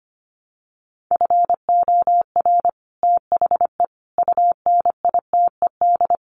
Morse code, heard as a keyed tone, about 25 wpm, 700 Hz. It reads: FOR THE UNITED
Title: Sets of 4 Words 25wpm